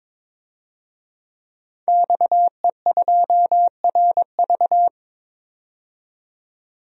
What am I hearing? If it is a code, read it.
Morse code, heard as a keyed tone, about 22 wpm, 700 Hz. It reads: XE2RV
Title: DXCC-100 Call Signs - 5 Characters - 22wpm